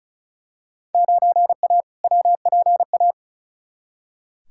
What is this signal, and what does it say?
Morse code, heard as a keyed tone, about 34 wpm, 700 Hz. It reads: 9A WPA
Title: Field Day Exchanges Encore - 35wpm